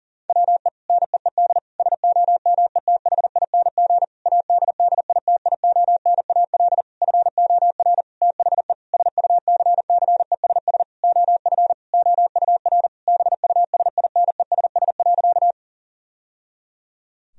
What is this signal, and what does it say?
Morse code, heard as a keyed tone, about 40 wpm, 700 Hz. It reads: WE NEED SOMETHING ADDITIONAL FOR THE SUCCESS OF OUR BUSINESS.
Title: Sentences from Top 1000 Words - Review Part 4 - 40wpm